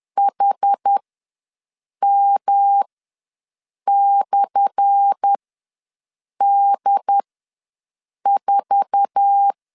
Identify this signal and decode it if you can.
Morse code, heard as a keyed tone, about 10 wpm, spaced as for 6 wpm, 800 Hz. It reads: HM/D4
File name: LEZIONI / 10 SIGNS / 10D COMA ws 10wpm 4314